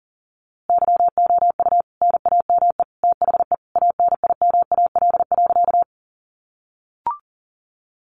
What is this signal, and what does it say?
Morse code, heard as a keyed tone, about 40 wpm, 700 Hz. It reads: YOU NAME THE ANIMAL.
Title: A Sound Of Thunder 40wpm